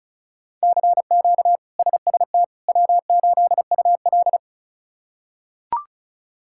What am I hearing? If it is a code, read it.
Morse code, heard as a keyed tone, about 34 wpm, 700 Hz. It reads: CQ SST W8UL
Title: K1USN SST QSO Practice - 35wpm